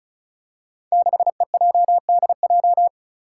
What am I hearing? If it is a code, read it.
Morse code, heard as a keyed tone, about 36 wpm, 700 Hz. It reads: 6EJDJ
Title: Five Challenging Letter-Number Combinations 35wpm